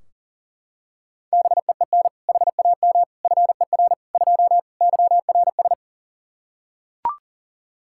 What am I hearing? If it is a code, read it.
Morse code, heard as a keyed tone, about 40 wpm, 700 Hz. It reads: BEEN HAM FER 2 YRS
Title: Practice QSO - Exchange 3 - More Info - 40wpm